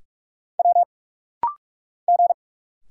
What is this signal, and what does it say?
Morse code, heard as a keyed tone, about 46 wpm, 690 Hz. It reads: W G
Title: Single Letters - Rapid-Fire 45wpm